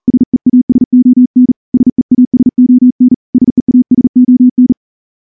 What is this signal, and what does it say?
Morse code, heard as a keyed tone, about 38 wpm, 250 Hz. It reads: SEASON SEASON SEASON
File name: morse-training-40WPM